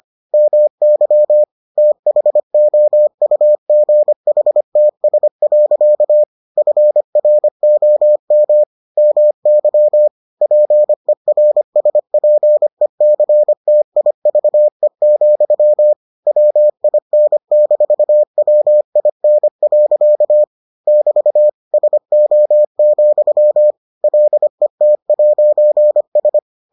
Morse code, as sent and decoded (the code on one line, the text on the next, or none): -- -.-- / - .... --- ..- --. .... - ... .-.-.- / ..-. .-. --- -- / -- -.-- / .--. .-. ... .--. -.-. - .. ...- . --..-- / .-- .. -. -....- .-- .. -. .-.-.- / -...- / ... --- --..-- / .-.. - .----. ...
MY THOUGHTS. FROM MY PERSPECTIVE, WIN-WIN. <BT> SO, LET'S